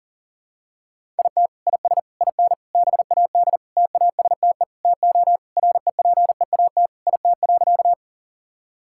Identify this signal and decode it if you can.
Morse code, heard as a keyed tone, about 40 wpm, 700 Hz. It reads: IT IS IN BAD TASTE TO REPEAT IT.
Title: Sentences from Top 1600 Words 40wpm